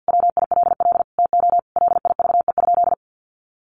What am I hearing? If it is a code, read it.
Morse code, heard as a keyed tone, about 50 wpm, 700 Hz. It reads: WILL TO LIVE?
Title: Sentences from Top 1500 Words 50wpm